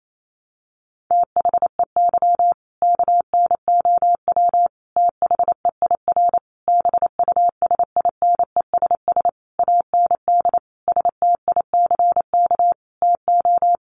Morse code, heard as a keyed tone, about 28 wpm, 700 Hz. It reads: THEY KNOW THEIR BUSINESS AND STICK TO
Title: Sentences from Top 1200 Words 28wpm